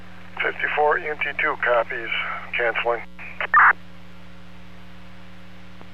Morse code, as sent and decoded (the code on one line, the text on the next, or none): none